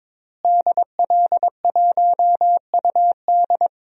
Morse code, heard as a keyed tone, about 22 wpm, 700 Hz. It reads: DL1UD
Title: DXCC-100 Call Signs - 5 Characters - Repeated 3x - 22wpm